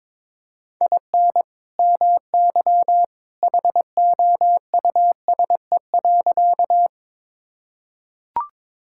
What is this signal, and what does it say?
Morse code, heard as a keyed tone, about 21 wpm, 700 Hz. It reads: IN MY HOUSE.
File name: practice / 22wpm / Sentences with 2-Word Phrases - 22wpm